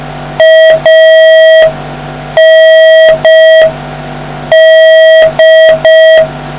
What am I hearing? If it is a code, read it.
Morse code, heard as a keyed tone, about 5 wpm, 1950 Hz, with noise. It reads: AND